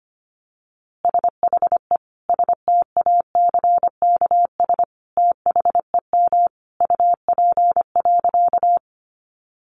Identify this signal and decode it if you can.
Morse code, heard as a keyed tone, about 25 wpm, 700 Hz. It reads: SHE STACKS THEM UP.
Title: Easy Phrases 25wpm